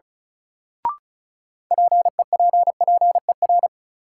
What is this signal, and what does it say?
Morse code, heard as a keyed tone, about 36 wpm, 700 Hz. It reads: PEPPER
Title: Sets of 5 Words - New Word Drill - 35wpm